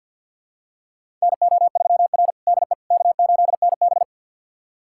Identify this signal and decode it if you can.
Morse code, heard as a keyed tone, about 49 wpm, 700 Hz. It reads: NO3R DE K8NB
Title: USA Callsign DE Callsign - 4 Characters - 50wpm